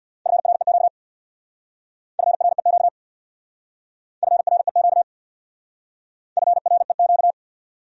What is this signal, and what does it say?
Morse code, heard as a keyed tone, about 69 wpm, 700 Hz. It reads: FREQ FREQ FREQ FREQ